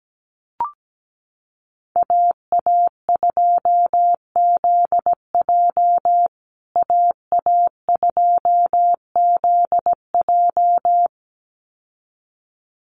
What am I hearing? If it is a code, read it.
Morse code, heard as a keyed tone, about 17 wpm, 700 Hz. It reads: AA2ZJ AA2ZJ
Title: USA Call Signs Repeated 2x 17wpm